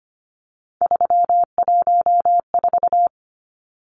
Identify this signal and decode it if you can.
Morse code, heard as a keyed tone, about 25 wpm, 690 Hz. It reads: 314
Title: Numbers - 3-Digit 25wpm